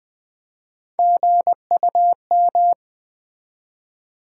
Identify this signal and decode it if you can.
Morse code, heard as a keyed tone, about 20 wpm, 700 Hz. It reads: GUM
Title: Easy Words with Hints - 20wpm